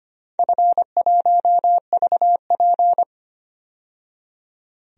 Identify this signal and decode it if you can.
Morse code, heard as a keyed tone, about 25 wpm, 700 Hz. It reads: F1VP
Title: DXCC-100 Call Signs - 4 Characters - 25wpm